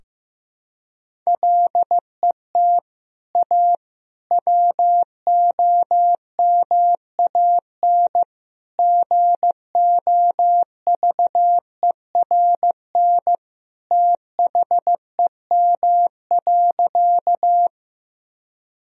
Morse code, heard as a keyed tone, about 15 wpm, 700 Hz. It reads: LET A WOMAN GOVERN THEM.
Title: Sentences from Top 1000 Words - Review Part 2 - 15wpm